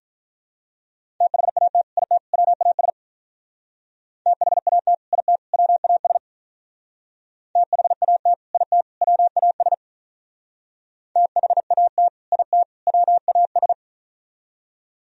Morse code, heard as a keyed tone, about 43 wpm, 700 Hz. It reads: THAT IT WAS THAT IT WAS THAT IT WAS THAT IT WAS
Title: Speed-Racing - Sets of 3 Words 35wpm